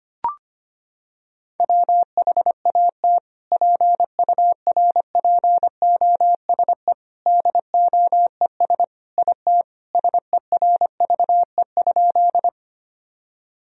Morse code, heard as a keyed tone, about 25 wpm, 700 Hz. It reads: WHAT PURPOSE DOES IT SERVE?
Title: Sentences from Top 1000 Words - Review Part 2 - 25wpm